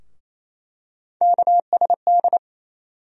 Morse code, heard as a keyed tone, about 28 wpm, 700 Hz. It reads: KSD
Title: USA Call Sign Postfixes - 3 Letters 28wpm